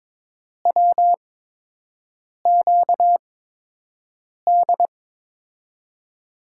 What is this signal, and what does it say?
Morse code, heard as a keyed tone, about 22 wpm, spaced as for 6 wpm, 700 Hz. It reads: WQD
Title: USA Call Signs - 5 Characters - 8x Character Spacing - 22wpm